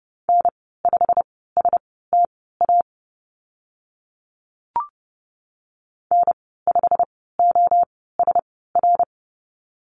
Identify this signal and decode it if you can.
Morse code, heard as a keyed tone, about 30 wpm, spaced as for 17 wpm, 700 Hz. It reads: N5STA N5OSR
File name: Lesson 15g - Number 5 - Call Signs 30wpm